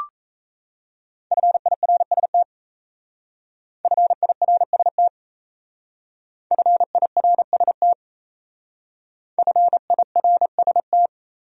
Morse code, heard as a keyed tone, about 35 wpm, 700 Hz. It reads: FIRST FIRST FIRST FIRST